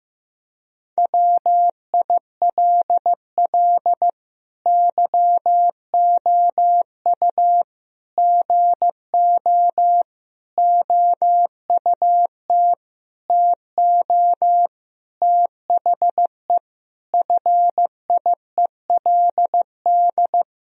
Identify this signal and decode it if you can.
Morse code, heard as a keyed tone, about 16 wpm, 700 Hz. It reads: WILL YOU GO OUT TO THE FIELD
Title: Sentences from Top 1000 Words - Review Part 2 - 15wpm